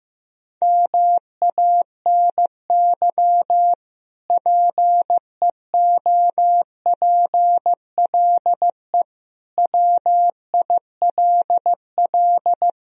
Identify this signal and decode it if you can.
Morse code, heard as a keyed tone, about 15 wpm, 700 Hz. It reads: MANY PEOPLE WILL
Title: Sentences from Top 1000 Words - Review Part 1 - 15wpm